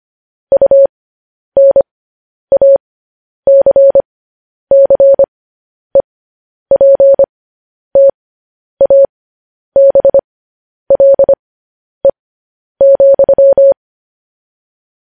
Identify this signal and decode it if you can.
Morse code, heard as a keyed tone, about 25 wpm, spaced as for 10 wpm, 550 Hz. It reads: UNACCEPTABLE,
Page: news headlines in morse code at 10 wpm